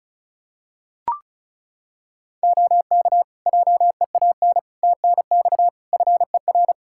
Morse code, heard as a keyed tone, about 34 wpm, 700 Hz. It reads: OK JEAN TNX FER
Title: Practice QSO - Exchange 4 - Ending - 35wpm